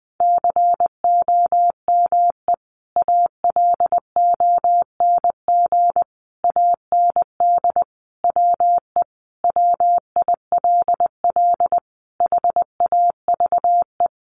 Morse code, heard as a keyed tone, about 20 wpm, 700 Hz. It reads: COME ALONG AND WE WILL HAVE
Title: Sentences from Top 1500 Words 20wpm